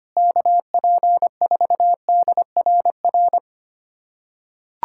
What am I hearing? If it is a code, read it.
Morse code, heard as a keyed tone, about 25 wpm, 700 Hz. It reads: KP4DRR
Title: USA Callsign DE Callsign - 6 Characters - 25wpm